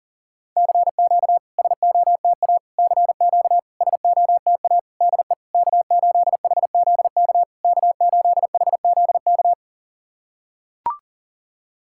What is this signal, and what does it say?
Morse code, heard as a keyed tone, about 41 wpm, 700 Hz. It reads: CQ SOTA CQ SOTA DE K8HZK K8HZK
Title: SOTA QSO Practice 40wpm